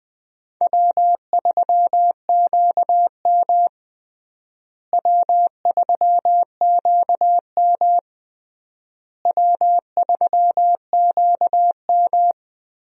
Morse code, heard as a keyed tone, about 20 wpm, 700 Hz. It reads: W3QM W3QM W3QM
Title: USA Call Signs - 4 Characters - Repeated 3x - 20wpm